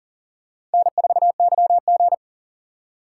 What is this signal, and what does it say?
Morse code, heard as a keyed tone, about 40 wpm, 700 Hz. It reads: N4YG
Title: USA Call Signs - 4 Characters - 3x Character Spacing - 40wpm